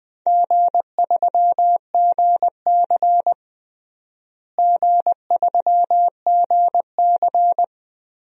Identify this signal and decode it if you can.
Morse code, heard as a keyed tone, about 20 wpm, 700 Hz. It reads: G3GC G3GC
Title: DXCC-100 Call Signs - 4 Characters - Repeated 2x - 20wpm